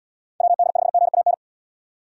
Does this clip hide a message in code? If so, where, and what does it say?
Yes: Morse code from 0.4 to 1.3 s, heard as a keyed tone, about 77 wpm, 700 Hz. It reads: WD5GAN